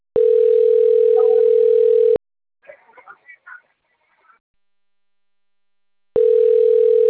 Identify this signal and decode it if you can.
Ringback tone (North America), 440 + 480 Hz.